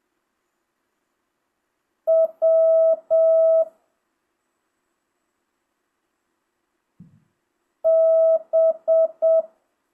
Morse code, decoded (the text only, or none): W B